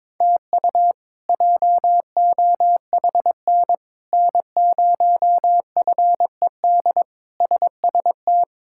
Morse code, heard as a keyed tone, about 21 wpm, 700 Hz. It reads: TU JOHN N0FED SST
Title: K1USN SST QSO Practice - 22wpm